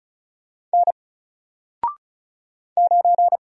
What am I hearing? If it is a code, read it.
Morse code, heard as a keyed tone, about 36 wpm, 700 Hz. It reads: N 9
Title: Single Letter-Number Alt 35wpm